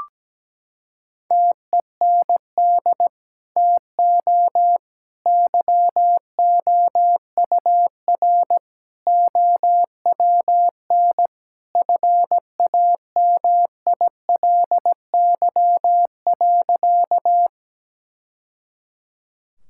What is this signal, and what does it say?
Morse code, heard as a keyed tone, about 17 wpm, 700 Hz. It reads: TEND TO YOUR OWN FAMILY.
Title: Sentences from Top 1000 Words - Review Part 5 - 17wpm